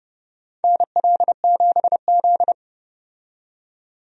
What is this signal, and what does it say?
Morse code, heard as a keyed tone, about 30 wpm, 700 Hz. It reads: NL7Z
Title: USA Call Signs - 4 Characters - 30wpm